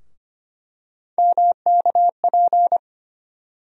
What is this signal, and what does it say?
Morse code, heard as a keyed tone, about 25 wpm, 700 Hz. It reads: MKP
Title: USA Call Sign Postfixes - 3 Letters 25wpm